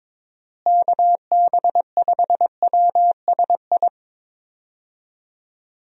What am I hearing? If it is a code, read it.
Morse code, heard as a keyed tone, about 22 wpm, 700 Hz. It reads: KB5WSI
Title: USA Call Signs - 6 Characters - Repeated 3x - 22wpm